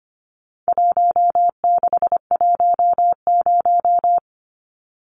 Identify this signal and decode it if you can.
Morse code, heard as a keyed tone, about 25 wpm, 700 Hz. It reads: 1610
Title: Numbers - 4-Digit 25wpm